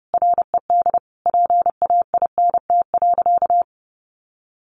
Morse code, heard as a keyed tone, about 30 wpm, 700 Hz. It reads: RED PAINT.
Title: A Sound Of Thunder 30wpm